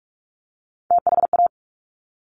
Morse code, heard as a keyed tone, about 46 wpm, 700 Hz. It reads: THA
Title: Top Three Letter Combinations in Words 45wpm